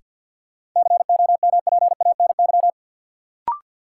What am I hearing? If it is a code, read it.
Morse code, heard as a keyed tone, about 53 wpm, 700 Hz. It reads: COMPANY
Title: Sets of 2 Words - New Word Drill - 50wpm